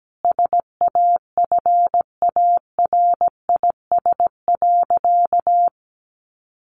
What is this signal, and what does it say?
Morse code, heard as a keyed tone, about 17 wpm, 700 Hz. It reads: SAFARIS.
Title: A Sound Of Thunder 17wpm